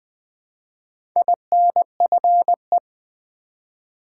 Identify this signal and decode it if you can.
Morse code, heard as a keyed tone, about 20 wpm, 700 Hz. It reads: INFE
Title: Build A Word - 1 Letter At A Time - 5-Letter Words - 20wpm